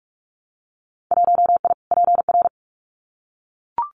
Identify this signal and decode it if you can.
Morse code, heard as a keyed tone, about 44 wpm, 700 Hz. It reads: 1I PR
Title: Winter Field Day Exchanges Encore 45wpm